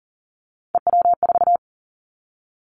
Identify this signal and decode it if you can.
Morse code, heard as a keyed tone, about 37 wpm, 700 Hz. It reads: EW4
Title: Callsign Prefixes for Top 100 DXCC - 40wpm